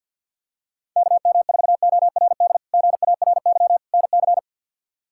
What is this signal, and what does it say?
Morse code, heard as a keyed tone, about 51 wpm, 700 Hz. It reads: KM4ORD GARY NC
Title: K1USN SST QSO Practice - 50wpm